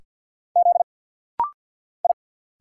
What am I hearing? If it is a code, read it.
Morse code, heard as a keyed tone, about 50 wpm, 690 Hz. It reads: Z I
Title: Single Letter-Number Alt - Rapid-Fire 50wpm